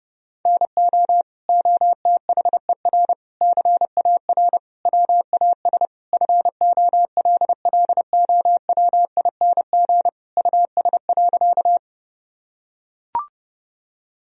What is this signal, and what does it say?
Morse code, heard as a keyed tone, about 30 wpm, 700 Hz. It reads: NO OTHER CAR WAS FOLLOWING US.
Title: Sentences from Top 1000 Words - Review Part 4 - 30wpm